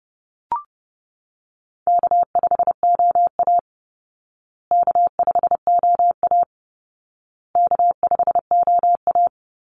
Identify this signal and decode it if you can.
Morse code, heard as a keyed tone, about 30 wpm, 700 Hz. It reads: K5OA K5OA K5OA
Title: Easy Call Signs Repeated 3x - 30wpm